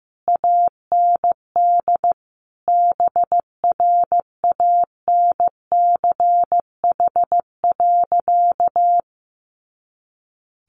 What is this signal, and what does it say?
Morse code, heard as a keyed tone, about 15 wpm, 700 Hz. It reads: AND BRANCH.
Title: A Sound Of Thunder 15wpm